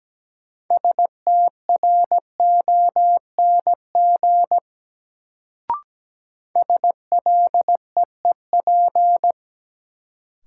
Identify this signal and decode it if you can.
Morse code, heard as a keyed tone, about 17 wpm, 700 Hz. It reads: STRONG SLEEP
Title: Sentences from Top 500 Words - New Word Drill - 17wpm